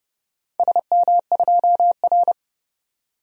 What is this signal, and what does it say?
Morse code, heard as a keyed tone, about 30 wpm, 700 Hz. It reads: SM2R
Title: DXCC-100 Call Signs - 4 Characters - 30wpm